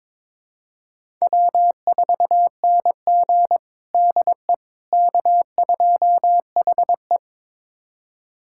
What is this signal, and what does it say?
Morse code, heard as a keyed tone, about 22 wpm, 700 Hz. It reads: W4NG DE K2HE
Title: USA Callsign DE Callsign - 4 Characters - 22wpm